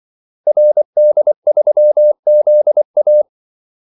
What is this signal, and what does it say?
Morse code, heard as a keyed tone, about 24 wpm, 600 Hz. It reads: RD3ZA
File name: Callsigns-24wpm